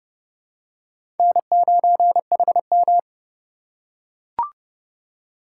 Morse code, heard as a keyed tone, about 30 wpm, 700 Hz. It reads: N9HM
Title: USA Call Signs - 4 Characters - 3x Character Spacing - 30wpm